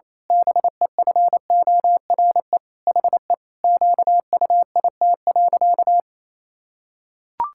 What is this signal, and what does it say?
Morse code, heard as a keyed tone, about 28 wpm, 700 Hz. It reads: BEFORE HE QUIT.
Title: Pangram Challenge v2 - 28wpm